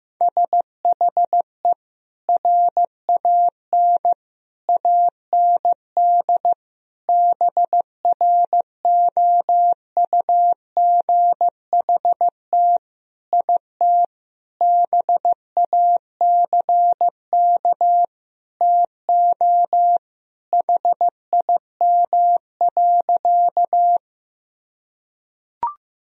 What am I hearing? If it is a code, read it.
Morse code, heard as a keyed tone, about 15 wpm, 700 Hz. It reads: SHE RAN AND BROUGHT IT BACK TO HIM.
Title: Sentences from Top 1000 Words - Review Part 3 - 15wpm